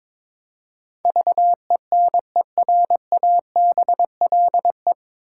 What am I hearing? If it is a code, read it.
Morse code, heard as a keyed tone, about 21 wpm, 700 Hz. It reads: VENERABLE
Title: Pangram Challenge v2 - 22wpm